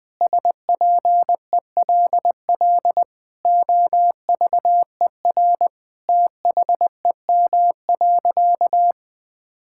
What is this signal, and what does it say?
Morse code, heard as a keyed tone, about 20 wpm, 700 Hz. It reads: SPELL OVER THEM.
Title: Sentences from Top 1000 Words - Review Part 6 - 20wpm